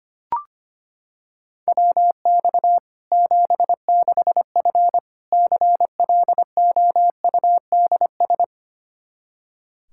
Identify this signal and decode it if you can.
Morse code, heard as a keyed tone, about 25 wpm, 700 Hz. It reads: WX 76F CLOUDS
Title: Practice QSO - Exchange 2 - Info - 25wpm